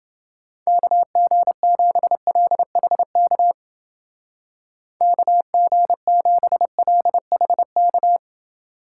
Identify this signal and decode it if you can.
Morse code, heard as a keyed tone, about 29 wpm, 700 Hz. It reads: KG7LHK KG7LHK